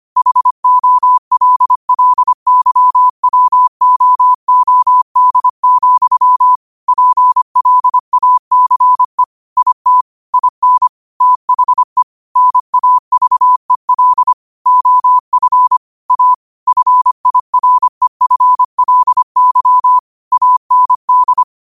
Morse code, heard as a keyed tone, about 25 wpm, 1000 Hz. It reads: HOLLYWOOD, PLACE IT IN THE NAVEL OF A FIREFLY AND
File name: qotdfiles / 25wpm